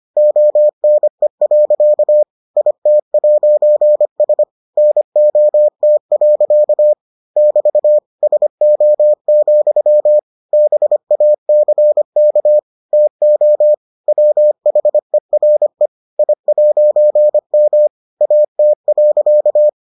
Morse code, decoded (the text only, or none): ONE. IT'S NOT. <BT> SO, BACK TO WHERE I'M AT.